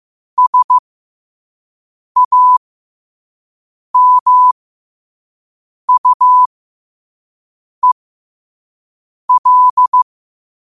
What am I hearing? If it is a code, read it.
Morse code, heard as a keyed tone, about 15 wpm, spaced as for 5 wpm, 1000 Hz. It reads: SAMUEL
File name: qotdfiles / 5wpm